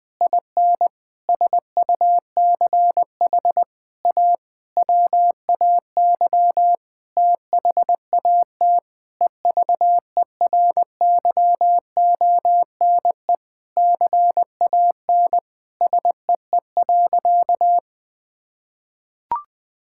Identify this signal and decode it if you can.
Morse code, heard as a keyed tone, about 20 wpm, 700 Hz. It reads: IN SUCH A WAY THAT EVERYONE CAN SEE.
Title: Sentences with 5-Word Phrases - 20wpm